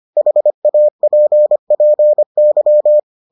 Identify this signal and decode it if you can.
Morse code, heard as a keyed tone, about 25 wpm, 600 Hz. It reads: HAPPY